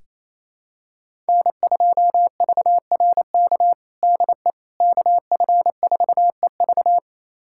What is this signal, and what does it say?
Morse code, heard as a keyed tone, about 27 wpm, 700 Hz. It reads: N2VRK DE KF4EV
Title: USA Callsign DE Callsign - 5 Characters - 28wpm